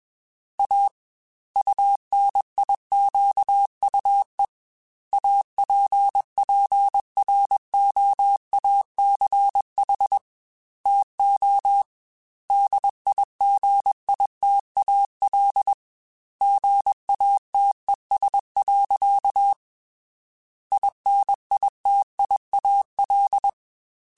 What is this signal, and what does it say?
Morse code, heard as a keyed tone, about 21 wpm, 800 Hz. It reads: A UNIQUE APPROACH TO DIGITAL GATES. INITIAL
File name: Technical Subjects / Technical Subjects-mp3s / Gates Intro @20